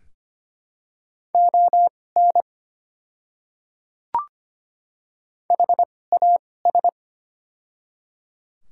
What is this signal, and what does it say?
Morse code, heard as a keyed tone, about 25 wpm, spaced as for 18 wpm, 690 Hz. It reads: ON HAS
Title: 2x Character Spacing - Top 200 Words - 25wpm